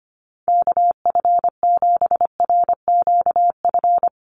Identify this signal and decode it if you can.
Morse code, heard as a keyed tone, about 25 wpm, 700 Hz. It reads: KF7RQF
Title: USA Call Signs Repeated 2x 25wpm